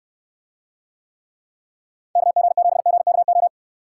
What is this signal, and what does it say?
Morse code, heard as a keyed tone, about 68 wpm, 700 Hz. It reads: KG7GKO